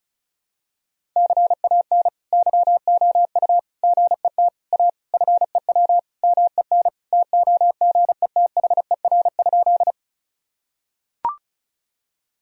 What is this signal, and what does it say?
Morse code, heard as a keyed tone, about 34 wpm, 700 Hz. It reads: CAN YOU GET A FEW MEN TOGETHER?
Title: Sentences from Top 1000 Words - Review Part 1 - 35wpm